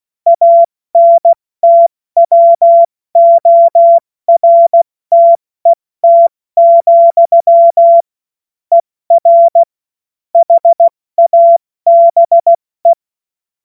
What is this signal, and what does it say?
Morse code, heard as a keyed tone, about 16 wpm, spaced as for 14 wpm, 680 Hz. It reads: ANTWORTET, ER HABE
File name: Witz-16-September-2025-70-F80